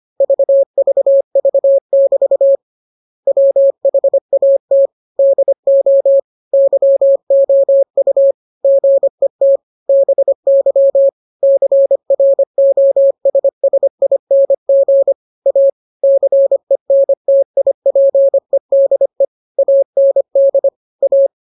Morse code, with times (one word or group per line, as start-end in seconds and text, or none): 0.2-2.6 VVV<BT>
3.3-4.9 WHAT
5.2-6.2 DO
6.5-8.3 YOU
8.6-9.6 GET
9.9-11.1 BY
11.4-15.1 CROSSING
15.5-15.7 A
16.0-19.2 CENTIPEDE
19.6-20.7 AND
21.0-21.3 A